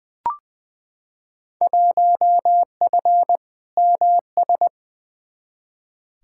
Morse code, heard as a keyed tone, about 20 wpm, 700 Hz. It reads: 1F MS
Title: Field Day Exchanges Encore - 20wpm